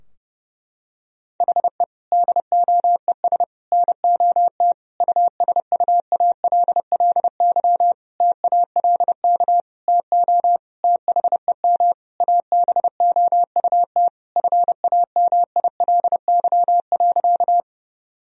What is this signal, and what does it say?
Morse code, heard as a keyed tone, about 32 wpm, 700 Hz. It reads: HE DOES NOT USUALLY TALK TO THEM ABOUT FAMILY.
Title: Sentences from Top 1000 Words - Review Part 3 - 30wpm